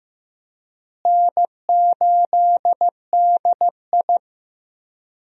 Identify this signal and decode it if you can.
Morse code, heard as a keyed tone, about 15 wpm, 700 Hz. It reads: N8DI
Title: USA Call Signs - 4 Characters - Repeated 3x - 15wpm